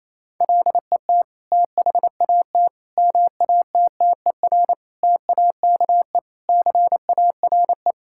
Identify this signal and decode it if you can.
Morse code, heard as a keyed tone, about 28 wpm, 700 Hz. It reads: LET THAT MATTER TAKE CARE
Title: Sentences from Top 1000 Words - Review Part 5 - 28wpm